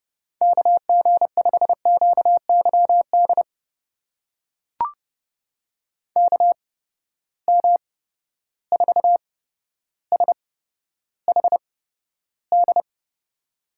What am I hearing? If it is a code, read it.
Morse code, heard as a keyed tone, about 30 wpm, spaced as for 13 wpm, 700 Hz. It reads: KG5QYD KM4SHD
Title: USA Call Signs - 6 Characters - 8x Character Spacing - 30wpm